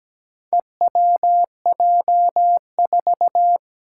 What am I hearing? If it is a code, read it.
Morse code, heard as a keyed tone, about 17 wpm, 700 Hz. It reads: EWJ4
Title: Four Challenging Letter-Number Combinations 17wpm